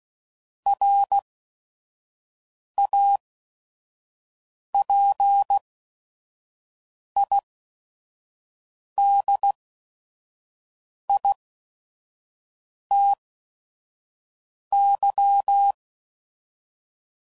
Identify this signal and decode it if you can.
Morse code, heard as a keyed tone, about 16 wpm, spaced as for 5 wpm, 800 Hz. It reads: RAPIDITY
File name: CW Code Groups @5WPM / Technical Subjects / XL